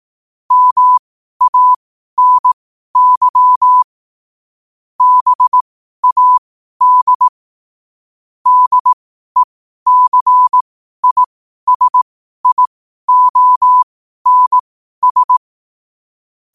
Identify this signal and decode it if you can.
Morse code, heard as a keyed tone, about 18 wpm, spaced as for 12 wpm, 1000 Hz. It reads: MANY BAD DECISIONS
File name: qotdfiles / 12wpm